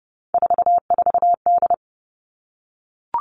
Morse code, heard as a keyed tone, about 30 wpm, 700 Hz. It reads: 44D
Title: All Cut Numbers - 3-Digit 30wpm